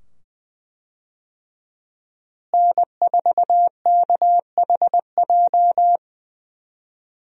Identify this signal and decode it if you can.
Morse code, heard as a keyed tone, about 20 wpm, 700 Hz. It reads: N4KHJ
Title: USA Call Signs - 5 Characters - Repeated 2x - 20wpm